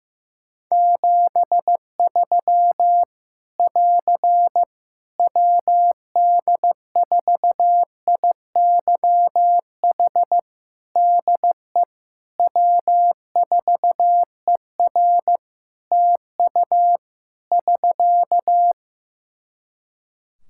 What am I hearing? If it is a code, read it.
Morse code, heard as a keyed tone, about 15 wpm, 700 Hz. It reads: 73 <AR> WD4IYH DE W4ER TU <SK>
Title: Practice QSO - Exchange 4 - Ending - 15wpm